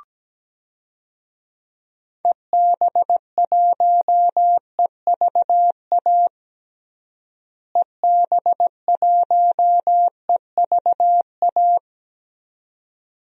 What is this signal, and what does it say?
Morse code, heard as a keyed tone, about 17 wpm, 700 Hz. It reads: EB1EVA EB1EVA
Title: DXCC-100 Call Signs - 6 Characters - Repeated 2x - 17wpm